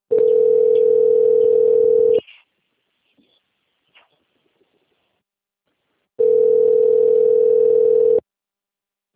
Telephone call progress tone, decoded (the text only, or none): ringback tone